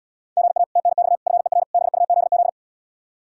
Morse code, heard as a keyed tone, about 79 wpm, 700 Hz. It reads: GA TNX FER BUZZ